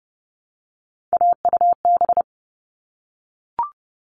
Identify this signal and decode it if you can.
Morse code, heard as a keyed tone, about 30 wpm, 700 Hz. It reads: AUB
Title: All Cut Numbers - 3-Digit 30wpm